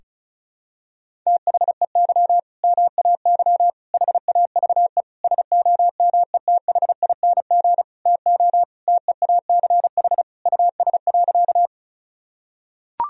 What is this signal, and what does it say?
Morse code, heard as a keyed tone, about 36 wpm, 700 Hz. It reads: THEY MAY HAVE SOMETHING TO TEACH US.
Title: Sentences from Top 1000 Words - Review Part 2 - 35wpm